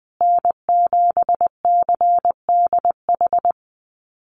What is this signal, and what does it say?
Morse code, heard as a keyed tone, about 20 wpm, 700 Hz. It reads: N7CDH
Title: USA Call Signs Repeated 3x 20wpm